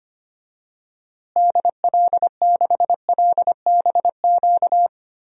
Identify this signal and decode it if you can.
Morse code, heard as a keyed tone, about 25 wpm, 700 Hz. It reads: DL6LBQ
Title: DXCC-100 Call Signs - 6 Characters - Repeated 3x - 25wpm